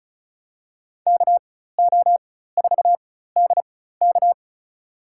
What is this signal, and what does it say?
Morse code, heard as a keyed tone, about 34 wpm, spaced as for 16 wpm, 700 Hz. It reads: KO4DK
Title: USA Call Signs - 5 Characters - 4x Character Spacing - 35wpm